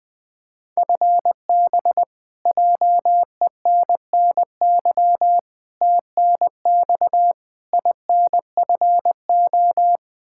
Morse code, heard as a keyed tone, about 21 wpm, 700 Hz. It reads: FB JENNY TNX INFO
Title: Practice QSO - Exchange 3 - More Info - 20wpm